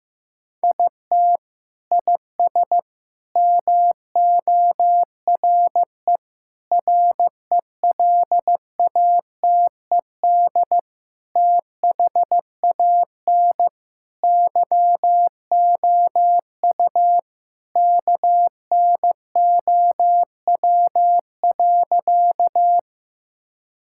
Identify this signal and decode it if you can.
Morse code, heard as a keyed tone, about 15 wpm, 700 Hz. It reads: IT IS MORE RELATED THAN YOU KNOW.
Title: Sentences from Top 1000 Words - Review Part 5 - 15wpm